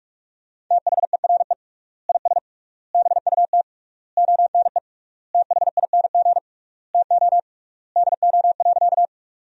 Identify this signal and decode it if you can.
Morse code, heard as a keyed tone, about 46 wpm, 700 Hz. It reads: THERE IS BUT ONE THING TO DO.
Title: 3x Word-Spacing - Sentences from Top 100 Words - 45wpm